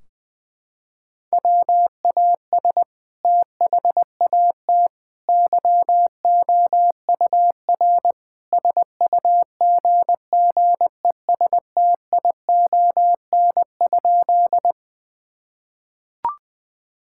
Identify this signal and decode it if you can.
Morse code, heard as a keyed tone, about 20 wpm, 700 Hz. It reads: WAS THAT YOUR SUGGESTION?
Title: Sentences from Top 1600 Words 20wpm